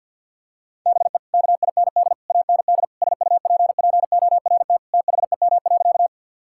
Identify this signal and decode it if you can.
Morse code, heard as a keyed tone, about 49 wpm, 700 Hz. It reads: BE KIND AND SUPPORT THEM.
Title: Sentences from Top 1000 Words - Review Part 5 - 50wpm